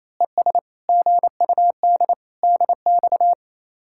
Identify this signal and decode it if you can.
Morse code, heard as a keyed tone, about 28 wpm, 700 Hz. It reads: ES GUD DX
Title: Speed-Racing Common QSO Phrases - 28wpm